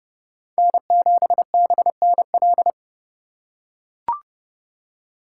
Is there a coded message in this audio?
Yes: Morse code from 0.6 to 2.7 s, heard as a keyed tone, about 30 wpm, 700 Hz. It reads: N7BNL